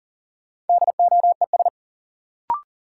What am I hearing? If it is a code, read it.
Morse code, heard as a keyed tone, about 40 wpm, 700 Hz. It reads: DOES
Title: Sets of 2 Words - New Word Drill - 40wpm